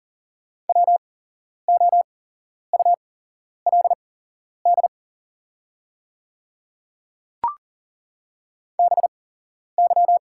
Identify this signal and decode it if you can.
Morse code, heard as a keyed tone, about 40 wpm, spaced as for 11 wpm, 690 Hz. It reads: WOULD BY
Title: 8x Character Spacing - Top 100 Words - 40wpm